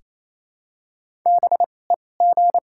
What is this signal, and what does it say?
Morse code, heard as a keyed tone, about 28 wpm, spaced as for 20 wpm, 700 Hz. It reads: BEG
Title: 2x Character Spacing - Top 500 Words - 28wpm